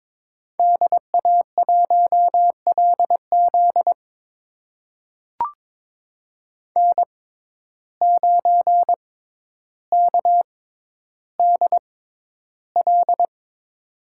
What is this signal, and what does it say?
Morse code, heard as a keyed tone, about 22 wpm, spaced as for 11 wpm, 700 Hz. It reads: DA1LZ N9KDL
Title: USA Call Signs - 5 Characters - 6x Character Spacing - 22wpm